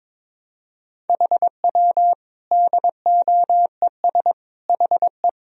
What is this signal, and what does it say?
Morse code, heard as a keyed tone, about 22 wpm, 700 Hz. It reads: HW DOES HE
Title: Sentences from Top 200 Words and Abbreviations 22wpm